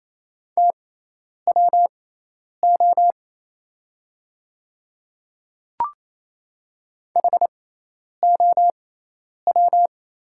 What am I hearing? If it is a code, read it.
Morse code, heard as a keyed tone, about 27 wpm, spaced as for 10 wpm, 690 Hz. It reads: TWO HOW
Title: 6x Character Spacing - Top 100 Words - 28wpm